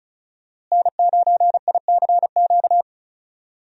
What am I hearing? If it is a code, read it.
Morse code, heard as a keyed tone, about 36 wpm, 700 Hz. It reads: N9ICQ
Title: USA Call Signs - 5 Characters - Repeated 2x - 35wpm